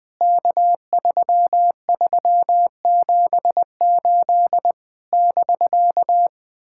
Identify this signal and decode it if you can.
Morse code, heard as a keyed tone, about 20 wpm, 700 Hz. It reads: K3378 <BK>
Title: POTA QSO Practice - 20wpm